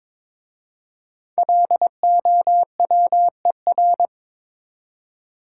Morse code, heard as a keyed tone, about 22 wpm, 690 Hz. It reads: LOWER
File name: 5 Letter Words - First 3 Letters Given - 22wpm